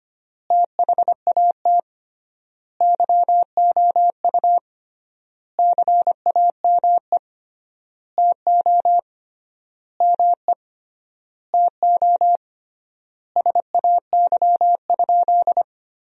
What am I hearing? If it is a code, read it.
Morse code, heard as a keyed tone, about 25 wpm, 700 Hz. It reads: THAT YOU CAME TO ME TO SAY?
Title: 3x Word-Spacing - Sentences from Top 200 Words - 25wpm